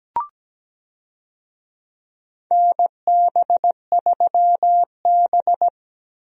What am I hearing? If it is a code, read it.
Morse code, heard as a keyed tone, about 17 wpm, 700 Hz. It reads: NB3B
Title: USA Call Signs - 4 Characters - Repeated 3x - 17wpm